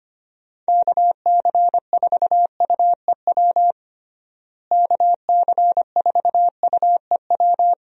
Morse code, heard as a keyed tone, about 25 wpm, 700 Hz. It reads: KC4UEW KC4UEW
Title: USA Call Signs - 6 Characters - Repeated 2x - 25wpm